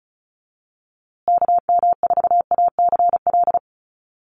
Morse code, heard as a keyed tone, about 34 wpm, 700 Hz. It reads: KM4ACL
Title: USA Call Signs Repeated 3x 35wpm